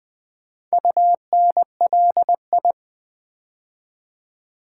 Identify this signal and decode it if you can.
Morse code, heard as a keyed tone, about 20 wpm, 700 Hz. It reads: UNLI
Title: Build A Word - 4 Letters At A Time - 8-Letter Words - 20wpm